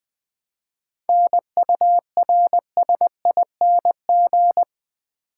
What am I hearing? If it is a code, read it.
Morse code, heard as a keyed tone, about 20 wpm, 700 Hz. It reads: NURSING
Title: 7 Letter Words - First 3 Letters Given - 20wpm